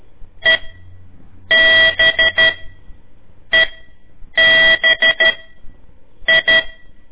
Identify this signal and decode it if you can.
Morse code, heard as a keyed tone, about 13 wpm, spaced as for 7 wpm, 1950 Hz. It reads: EBEBI
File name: ik1weg2 / CORSI CW / lez4-1